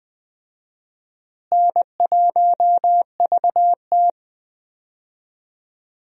Morse code, heard as a keyed tone, about 20 wpm, 700 Hz. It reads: N1VT
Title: USA Call Signs - 4 Characters - Repeated 2x - 20wpm